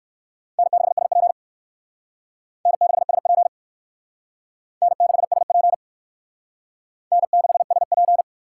Morse code, heard as a keyed tone, about 56 wpm, 700 Hz. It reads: N6SP N6SP N6SP N6SP
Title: Speed Racing - USA Call Signs 45wpm